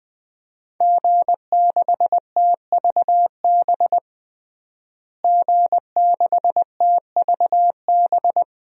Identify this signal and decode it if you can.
Morse code, heard as a keyed tone, about 20 wpm, 700 Hz. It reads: G6TVB G6TVB
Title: DXCC-100 Call Signs - 5 Characters - Repeated 3x - 20wpm